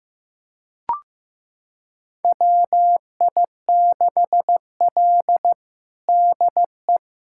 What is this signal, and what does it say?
Morse code, heard as a keyed tone, about 15 wpm, 700 Hz. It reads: WI6L DE
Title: USA Callsign DE Callsign - 4 Characters - 15wpm